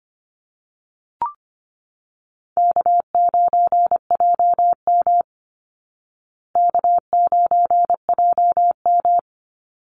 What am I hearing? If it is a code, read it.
Morse code, heard as a keyed tone, about 24 wpm, 700 Hz. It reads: K9JM K9JM
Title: Easy Call Signs Repeated 3x - 25wpm